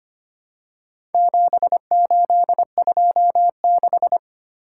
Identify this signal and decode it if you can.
Morse code, heard as a keyed tone, about 25 wpm, 700 Hz. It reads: 7826
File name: practice / 25wpm / Four Letter-Number Combinations - 25wpm